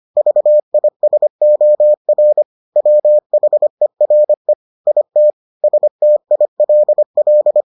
Morse code, heard as a keyed tone, about 25 wpm, 600 Hz. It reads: VISOR WHERE IT STILL